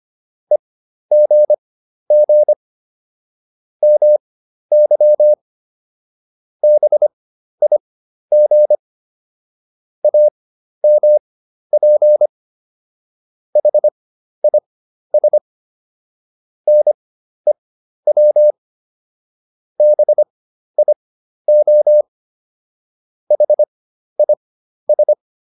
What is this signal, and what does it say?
Morse code, heard as a keyed tone, about 25 wpm, spaced as for 12 wpm, 600 Hz. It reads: EGG MY BIG AMP HIS NEW BIO HIS